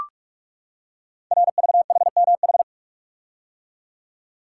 Morse code, heard as a keyed tone, about 44 wpm, 700 Hz. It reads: RVHMH
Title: Code Groups - 5 Letters - 45wpm